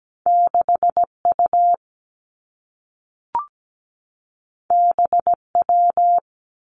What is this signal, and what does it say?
Morse code, heard as a keyed tone, about 17 wpm, 700 Hz. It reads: 6U BW
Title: Two Challenging Letter-Number Combinations 17wpm